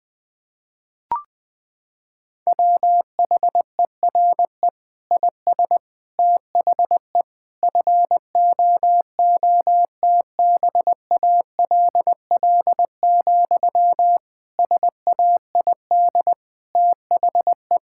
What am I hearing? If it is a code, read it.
Morse code, heard as a keyed tone, about 19 wpm, 700 Hz. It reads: WHERE IS THE FOOTBALL, SAID THE
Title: Sentences from Top 1600 Words 20wpm